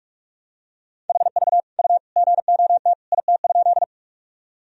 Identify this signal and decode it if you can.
Morse code, heard as a keyed tone, about 46 wpm, 700 Hz. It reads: HV U GOT IT?
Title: Sentences from Top 300 Words and Abbreviations 45wpm